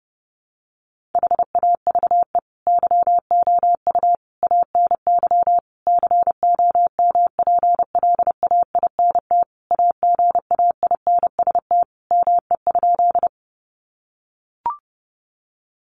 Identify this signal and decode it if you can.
Morse code, heard as a keyed tone, about 30 wpm, 700 Hz. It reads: HAVE YOU ANY COMPLAINT AGAINST ME?
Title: Sentences from Top 1400 Words 30wpm